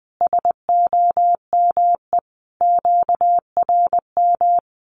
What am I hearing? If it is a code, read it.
Morse code, heard as a keyed tone, about 20 wpm, 700 Hz. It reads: SOME QRM
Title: Common QSO Phrases 20wpm